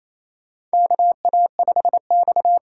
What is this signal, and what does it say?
Morse code, heard as a keyed tone, about 28 wpm, 700 Hz. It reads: KA5X
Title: USA Call Signs - 4 Characters - 4x Character Spacing - 28wpm